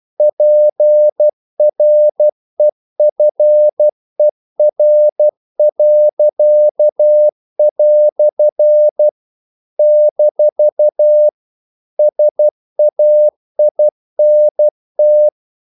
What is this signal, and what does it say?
Morse code, heard as a keyed tone, about 12 wpm, 600 Hz. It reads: PREFER." - SAINT
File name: CW / Practice / EnglishQuotations0112